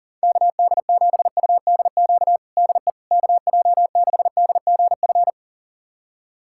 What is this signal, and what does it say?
Morse code, heard as a keyed tone, about 37 wpm, 700 Hz. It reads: KD7UDQ DE KJ6DGF
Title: USA Callsign DE Callsign - 6 Characters - 40wpm